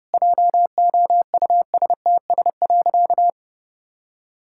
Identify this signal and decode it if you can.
Morse code, heard as a keyed tone, about 30 wpm, 700 Hz. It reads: JOUSTS.
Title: Pangram Challenge v2 - 30wpm